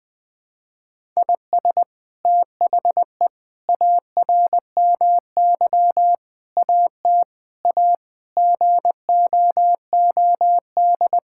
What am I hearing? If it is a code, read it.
Morse code, heard as a keyed tone, about 20 wpm, 700 Hz. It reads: IS THE ARMY AT A GOOD
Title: Sentences from Top 1000 Words - Review Part 6 - 20wpm